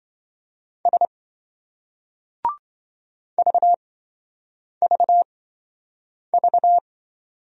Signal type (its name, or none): Morse code